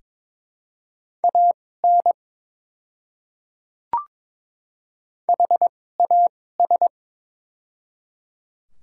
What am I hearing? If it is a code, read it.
Morse code, heard as a keyed tone, about 22 wpm, spaced as for 16 wpm, 700 Hz. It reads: AN HAS
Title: 2x Character Spacing - Top 100 Words - 22wpm